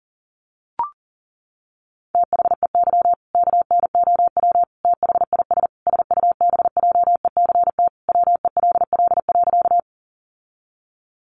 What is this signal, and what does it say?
Morse code, heard as a keyed tone, about 40 wpm, 700 Hz. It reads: THEY KNOW THIS SUBJECT WELL.
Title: Sentences from Top 800 Words 40wpm